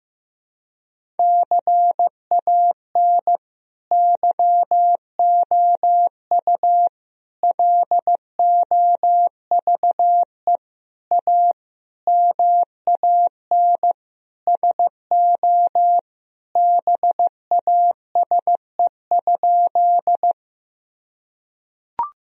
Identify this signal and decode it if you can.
Morse code, heard as a keyed tone, about 15 wpm, 700 Hz. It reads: CAN YOU LOVE A MAN SO BASE?
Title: Sentences from Top 1000 Words - Review Part 1 - 15wpm